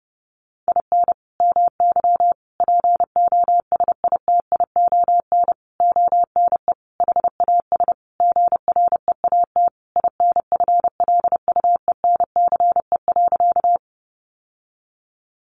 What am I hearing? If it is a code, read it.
Morse code, heard as a keyed tone, about 30 wpm, 700 Hz. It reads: IN MY POSITION ONE HAS GREAT INFLUENCE.
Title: Sentences from Top 1100 Words 30wpm